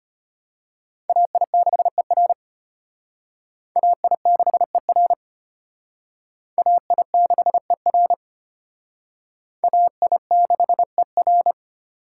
Morse code, heard as a keyed tone, about 34 wpm, 700 Hz. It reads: AI6ER AI6ER AI6ER AI6ER